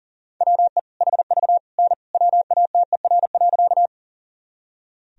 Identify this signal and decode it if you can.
Morse code, heard as a keyed tone, about 40 wpm, 700 Hz. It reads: WE HV N WATER.
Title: Sentences from Top 100 Words and Abbreviations 40wpm